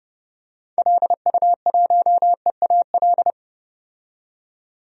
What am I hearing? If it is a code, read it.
Morse code, heard as a keyed tone, about 30 wpm, 700 Hz. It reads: LU1EAL
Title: DXCC-100 Call Signs - 6 Characters - 30wpm